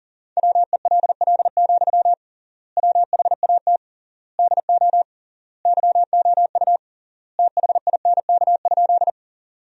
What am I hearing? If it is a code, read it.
Morse code, heard as a keyed tone, about 37 wpm, 700 Hz. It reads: WELL, WHAT DO YOU THINK?
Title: 3x Word-Spacing - Sentences from Top 200 Words - 40wpm